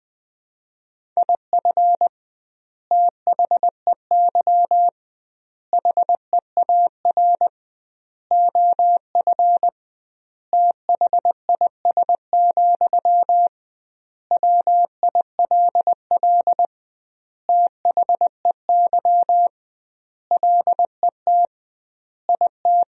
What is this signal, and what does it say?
Morse code, heard as a keyed tone, about 21 wpm, 700 Hz. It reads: IF THEY HEAR OF THIS, WILL THEY LET IT
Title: 2x Word-Spacing - Sentences from Top 400 Words - 20wpm